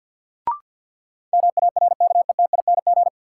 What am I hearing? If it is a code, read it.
Morse code, heard as a keyed tone, about 52 wpm, 700 Hz. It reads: MARKETING
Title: Sentences from Top 1000 Words - Review Part 5 - New Word Drill - 50wpm